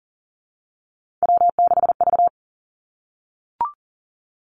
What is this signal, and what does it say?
Morse code, heard as a keyed tone, about 39 wpm, 700 Hz. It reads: W6V
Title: Three Challenging Letter-Number Combinations 40wpm